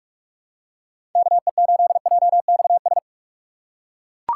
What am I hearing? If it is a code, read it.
Morse code, heard as a keyed tone, about 41 wpm, 700 Hz. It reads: KE8JXS
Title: USA Call Signs - 6 Characters - 2x Character Spacing - 45wpm